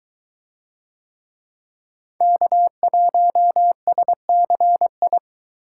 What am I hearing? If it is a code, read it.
Morse code, heard as a keyed tone, about 21 wpm, 700 Hz. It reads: K1SCI